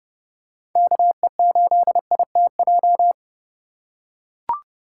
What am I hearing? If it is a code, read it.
Morse code, heard as a keyed tone, about 30 wpm, 700 Hz. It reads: KE8ITJ